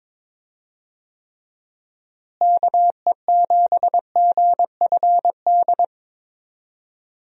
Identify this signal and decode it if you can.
Morse code, heard as a keyed tone, about 22 wpm, 700 Hz. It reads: KE7GFD
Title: USA Call Signs - 6 Characters - Repeated 3x - 22wpm